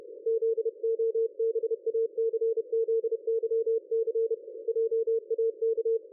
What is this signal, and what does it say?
Morse code, heard as a keyed tone, about 32 wpm, 450 Hz. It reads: ZOBACZYC JAK
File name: dlazaawansowanych / nagraniazpasma / ExpertSDR2 line out 23.01.2020 14.22.53